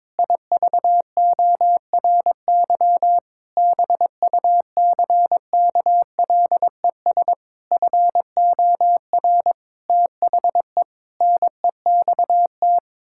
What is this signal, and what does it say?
Morse code, heard as a keyed tone, about 22 wpm, 700 Hz. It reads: IVORY BUCKLES FOR THE NEXT
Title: Pangram Challenge v2 - 22wpm